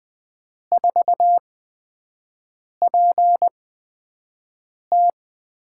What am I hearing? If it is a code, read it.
Morse code, heard as a keyed tone, about 20 wpm, spaced as for 5 wpm, 700 Hz. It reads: 4PT
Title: USA Call Signs - 6 Characters - 8x Character Spacing - 20wpm